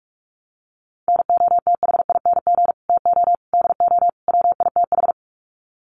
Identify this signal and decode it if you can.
Morse code, heard as a keyed tone, about 44 wpm, 700 Hz. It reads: NOTHING TO DO WITH
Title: Sets of 4 Words 45wpm